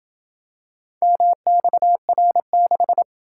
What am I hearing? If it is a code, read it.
Morse code, heard as a keyed tone, about 27 wpm, 700 Hz. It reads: MXR6